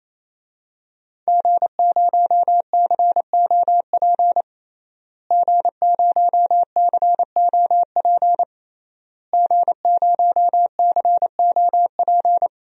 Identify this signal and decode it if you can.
Morse code, heard as a keyed tone, about 28 wpm, 700 Hz. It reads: G0COP G0COP G0COP
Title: DXCC-100 Call Signs - 5 Characters - Repeated 3x - 28wpm